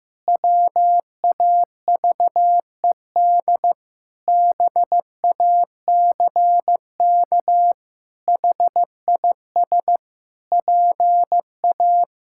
Morse code, heard as a keyed tone, about 15 wpm, 700 Hz. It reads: WAVED BACK HIS PA
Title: Pangram Challenge v2 - 15wpm